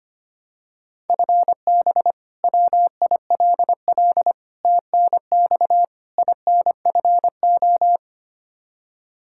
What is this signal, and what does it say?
Morse code, heard as a keyed tone, about 25 wpm, 700 Hz. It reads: FB WILL TNX INFO
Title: Practice QSO - Exchange 3 - More Info - 25wpm